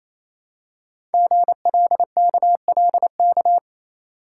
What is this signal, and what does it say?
Morse code, heard as a keyed tone, about 29 wpm, 700 Hz. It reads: GLKLK
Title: Code Groups - 5 Letters - 28wpm